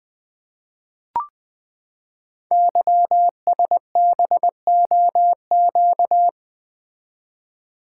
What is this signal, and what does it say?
Morse code, heard as a keyed tone, about 20 wpm, 700 Hz. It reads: YSBOQ
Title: Code Groups - 5 Letters - 20wpm